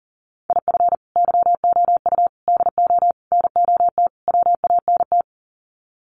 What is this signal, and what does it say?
Morse code, heard as a keyed tone, about 40 wpm, 700 Hz. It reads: IF YOU DO NOT WANT
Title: Sets of 5 Words 40wpm